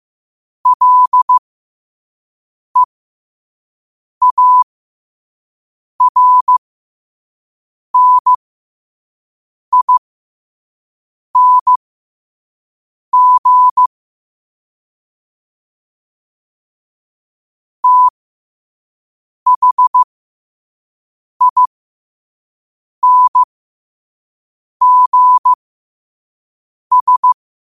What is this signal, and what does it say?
Morse code, heard as a keyed tone, about 15 wpm, spaced as for 5 wpm, 1000 Hz. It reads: LEARNING THINGS
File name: qotdfiles / 5wpm